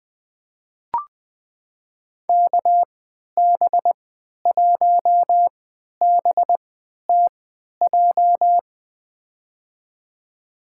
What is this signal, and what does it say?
Morse code, heard as a keyed tone, about 20 wpm, spaced as for 11 wpm, 700 Hz. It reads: KB1BTJ
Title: USA Call Signs - 6 Characters - 3x Character Spacing - 20wpm